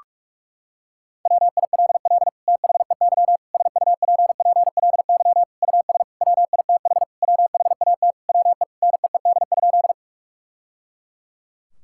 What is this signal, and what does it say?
Morse code, heard as a keyed tone, about 46 wpm, 700 Hz. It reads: WILL THEY SUPPLY US WITH WHAT WE NEED?
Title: Sentences from Top 1000 Words - Review Part 5 - 45wpm